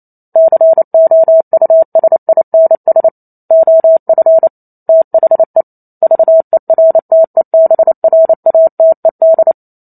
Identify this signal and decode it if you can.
Morse code, heard as a keyed tone, about 28 wpm, 650 Hz. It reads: COUSINS OF THE VERTEBRATED